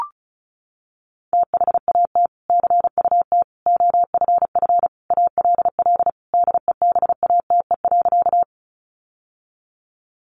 Morse code, heard as a keyed tone, about 34 wpm, 700 Hz. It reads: THAT CUT OFF ALL DEBATE.
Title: Sentences from Top 1100 Words 35wpm